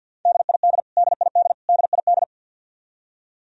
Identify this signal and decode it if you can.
Morse code, heard as a keyed tone, about 52 wpm, 700 Hz. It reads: DID DID DID
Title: Top 100 Words Repeated 50wpm